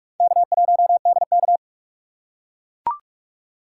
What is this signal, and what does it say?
Morse code, heard as a keyed tone, about 43 wpm, 700 Hz. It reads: K1DK
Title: USA Call Signs - 4 Characters - 3x Character Spacing - 45wpm